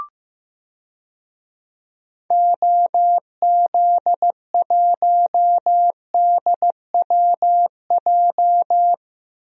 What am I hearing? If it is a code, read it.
Morse code, heard as a keyed tone, about 16 wpm, 700 Hz. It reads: OZ1DWJ
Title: DXCC-100 Call Signs - 6 Characters - 15wpm